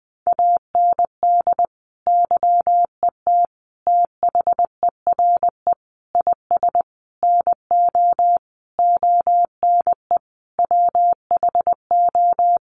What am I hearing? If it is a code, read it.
Morse code, heard as a keyed tone, about 20 wpm, 700 Hz. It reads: AND YET THERE IS NO ONE WHO
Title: Sentences from Top 1000 Words 20wpm